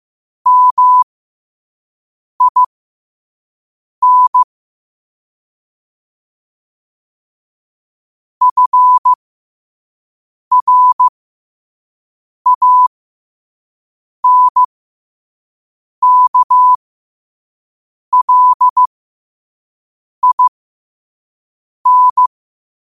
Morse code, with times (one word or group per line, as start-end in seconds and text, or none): 0.5-4.4 MIN
8.4-22.3 FRANKLIN